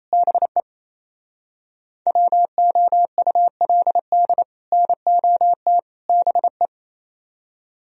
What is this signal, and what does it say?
Morse code, heard as a keyed tone, about 34 wpm, 700 Hz. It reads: BE WOULD NOT BE
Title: Speed-Racing - Sets of 3 Words 28wpm